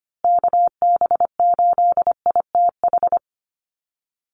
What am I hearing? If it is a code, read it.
Morse code, heard as a keyed tone, about 25 wpm, 700 Hz. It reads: KB8ITH
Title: USA Call Signs Repeated 2x 25wpm